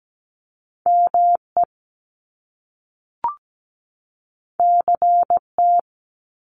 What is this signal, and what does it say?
Morse code, heard as a keyed tone, about 17 wpm, 700 Hz. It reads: ME CT
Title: US States 17wpm